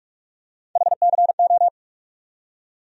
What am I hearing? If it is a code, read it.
Morse code, heard as a keyed tone, about 47 wpm, 700 Hz. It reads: HCO
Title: USA Call Sign Postfixes - 3 Letters 45wpm